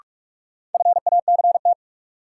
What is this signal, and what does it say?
Morse code, heard as a keyed tone, about 46 wpm, 700 Hz. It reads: FACT
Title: Sentences from Top 500 Words - New Word Drill - 45wpm